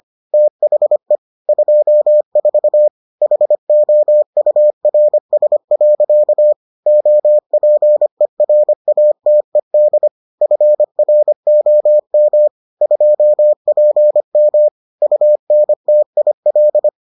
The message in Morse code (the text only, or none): THE 24 HOURS. OPERATED FROM 2PM UNTIL